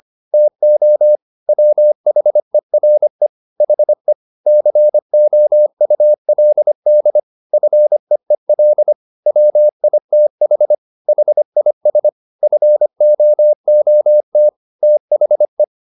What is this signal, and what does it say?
Morse code, heard as a keyed tone, about 25 wpm, 600 Hz. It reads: TO WHERE HE COULD FEEL WITH HIS FOOT THE